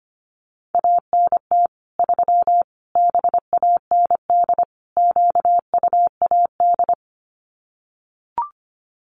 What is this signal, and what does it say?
Morse code, heard as a keyed tone, about 25 wpm, 700 Hz. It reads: ANT 3 BAND QUAD
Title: Common QSO Phrases 25wpm